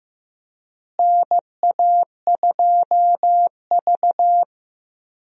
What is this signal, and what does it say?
Morse code, heard as a keyed tone, about 15 wpm, 700 Hz. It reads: NA2V
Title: USA Call Signs - 4 Characters - Repeated 3x - 15wpm